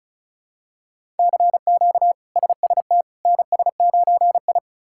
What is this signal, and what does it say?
Morse code, heard as a keyed tone, about 36 wpm, 700 Hz. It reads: CQ SST NS9I
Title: K1USN SST QSO Practice - 35wpm